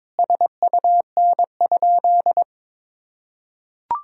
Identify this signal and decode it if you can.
Morse code, heard as a keyed tone, about 22 wpm, 700 Hz. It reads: SUN?
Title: Sentences from Top 1000 Words - Review Part 1 - 22wpm